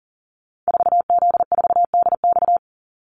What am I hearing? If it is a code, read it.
Morse code, heard as a keyed tone, about 40 wpm, 700 Hz. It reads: 4Z4DX
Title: Call Signs Repeated 2x - 40wpm